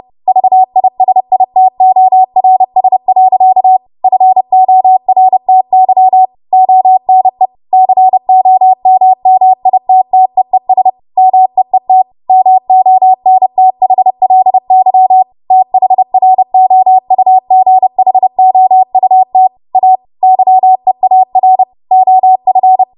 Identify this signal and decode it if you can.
Morse code, heard as a keyed tone, about 30 wpm, 750 Hz. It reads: VISITORS. FORTY ONE COMMITTEES MEET MONTHLY THROUGHOUT A YEAR OF
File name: Morse / 30WPM